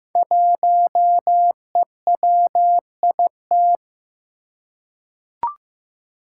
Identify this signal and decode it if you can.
Morse code, heard as a keyed tone, about 15 wpm, 700 Hz. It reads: 1EWIT
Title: Five Challenging Letter-Number Combinations 15wpm